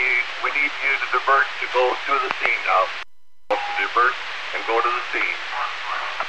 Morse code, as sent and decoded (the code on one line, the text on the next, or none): none